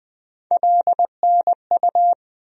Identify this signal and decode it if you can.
Morse code, heard as a keyed tone, about 21 wpm, 700 Hz. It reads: LNU
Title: USA Call Sign Postfixes - 3 Letters 20wpm